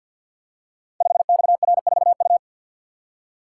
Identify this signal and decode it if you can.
Morse code, heard as a keyed tone, about 52 wpm, 700 Hz. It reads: 5XR4U
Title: Code Groups - 5 Letter:Numbers - 50wpm